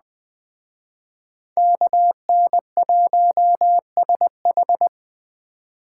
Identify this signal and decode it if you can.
Morse code, heard as a keyed tone, about 21 wpm, 700 Hz. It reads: KN1SH
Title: Speed Racing - USA 2x2 Call Signs 17wpm